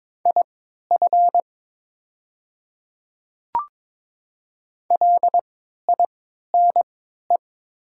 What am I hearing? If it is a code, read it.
Morse code, heard as a keyed tone, about 22 wpm, spaced as for 12 wpm, 700 Hz. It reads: IF LINE
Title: 3x Character Spacing - Top 300 Words - 22wpm